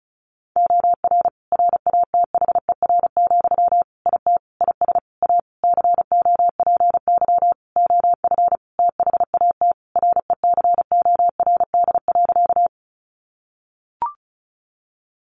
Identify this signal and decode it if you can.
Morse code, heard as a keyed tone, about 34 wpm, 700 Hz. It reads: OR RATHER, IT IS A COPY OF THAT RECORD.
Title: Sentences from Top 1400 Words 35wpm